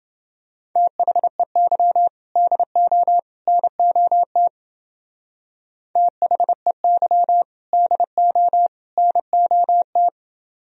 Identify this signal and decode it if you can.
Morse code, heard as a keyed tone, about 28 wpm, 700 Hz. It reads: THEY DO NOT THEY DO NOT